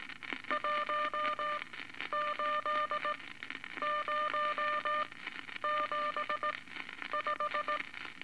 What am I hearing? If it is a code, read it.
Morse code, heard as a keyed tone, about 18 wpm, spaced as for 10 wpm, 1200 Hz, with noise. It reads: 18075
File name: Eight Words Per Minute